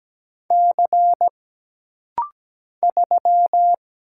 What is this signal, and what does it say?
Morse code, heard as a keyed tone, about 17 wpm, 700 Hz. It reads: C 3
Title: Single Letter-Number Alt - Rapid-Fire 17wpm